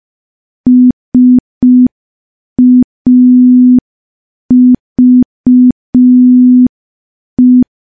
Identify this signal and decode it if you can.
Morse code, heard as a keyed tone, about 5 wpm, 250 Hz. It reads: SAVE
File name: morse / morse-training-05WPM